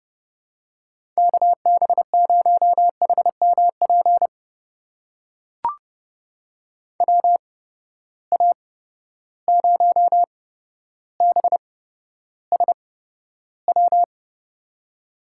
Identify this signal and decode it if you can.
Morse code, heard as a keyed tone, about 30 wpm, spaced as for 13 wpm, 700 Hz. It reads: KB0HMP WA0BSW
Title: USA Call Signs - 6 Characters - 8x Character Spacing - 30wpm